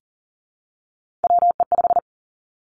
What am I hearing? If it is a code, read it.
Morse code, heard as a keyed tone, about 42 wpm, 700 Hz. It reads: WE5
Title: Three Challenging Letter-Number Combinations 40wpm